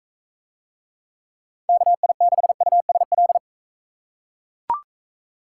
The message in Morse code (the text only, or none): KI6USL